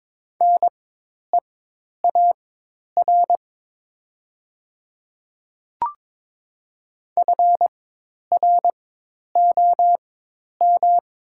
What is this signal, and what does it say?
Morse code, heard as a keyed tone, about 22 wpm, spaced as for 10 wpm, 700 Hz. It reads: NEAR FROM
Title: 4x Character Spacing - Top 200 Words - 22wpm